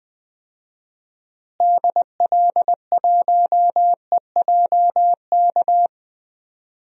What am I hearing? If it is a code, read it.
Morse code, heard as a keyed tone, about 20 wpm, 700 Hz. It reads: DL1EJK
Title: DXCC-100 Call Signs - 6 Characters - Repeated 3x - 20wpm